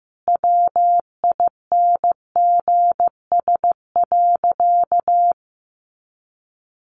Morse code, heard as a keyed tone, about 15 wpm, 700 Hz. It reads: WINGS.
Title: Additional Easy Phrases 15wpm